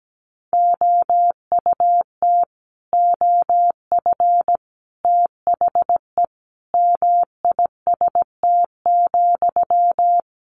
Morse code, heard as a keyed tone, about 17 wpm, 700 Hz. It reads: OUT OF THE MIST,
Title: A Sound Of Thunder 17wpm